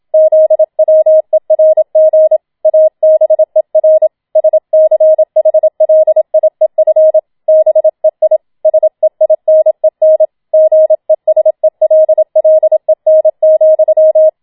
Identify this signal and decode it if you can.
Morse code, heard as a keyed tone, about 27 wpm, 630 Hz. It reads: ZWERG ABER SCHLIEF BEI SEINEN GESELLEN,